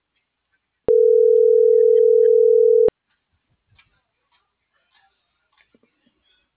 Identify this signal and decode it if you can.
Ringback tone (North America), 440 + 480 Hz.